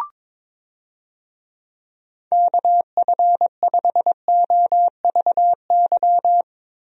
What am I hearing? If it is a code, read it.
Morse code, heard as a keyed tone, about 22 wpm, 700 Hz. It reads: KF5OVY
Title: USA Call Signs - 6 Characters - Repeated 2x - 22wpm